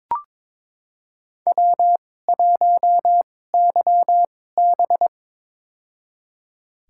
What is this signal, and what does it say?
Morse code, heard as a keyed tone, about 22 wpm, spaced as for 16 wpm, 700 Hz. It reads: W1YB
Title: USA Call Signs - 4 Characters - 2x Character Spacing - 22wpm